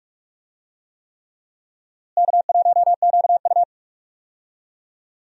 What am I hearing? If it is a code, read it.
Morse code, heard as a keyed tone, about 46 wpm, 700 Hz. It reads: K1QU
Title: USA Call Signs - 4 Characters - Repeated 3x - 45wpm